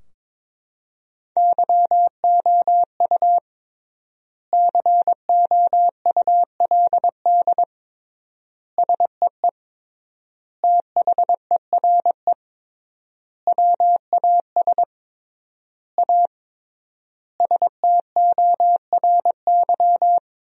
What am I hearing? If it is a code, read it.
Morse code, heard as a keyed tone, about 22 wpm, 700 Hz. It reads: YOU COULD SEE THERE WAS A STORY
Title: 3x Word-Spacing - Sentences from Top 500 Words - 22wpm